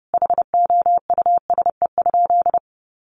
Morse code, heard as a keyed tone, about 30 wpm, 700 Hz. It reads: HOUSE?
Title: Sentences from Top 700 Words 30wpm